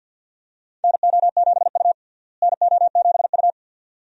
Morse code, heard as a keyed tone, about 52 wpm, 700 Hz. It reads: NO7U NO7U
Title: USA Call Signs - 4 Characters - Repeated 2x - 50wpm